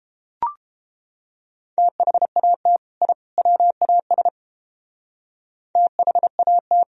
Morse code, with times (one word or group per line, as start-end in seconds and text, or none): 1.8-2.8 THAT
3.0-3.1 I
3.4-4.3 WAS
5.8-6.8 THAT